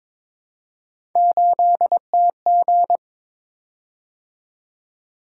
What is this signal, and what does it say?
Morse code, heard as a keyed tone, about 22 wpm, 700 Hz. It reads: 8TG
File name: practice / 22wpm / Three Letter-Number Combinations - 22wpm